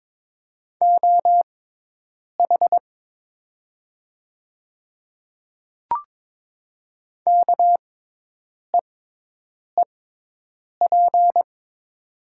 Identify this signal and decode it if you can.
Morse code, heard as a keyed tone, about 22 wpm, spaced as for 8 wpm, 700 Hz. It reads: OH KEEP
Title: 6x Character Spacing - Top 500 Words - 22wpm